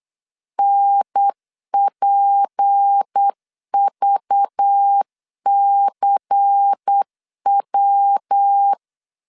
Morse code, heard as a keyed tone, about 8 wpm, 800 Hz. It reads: NPVCW
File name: LEZIONI / 6 PKX0 / 6 P 8wpm 3025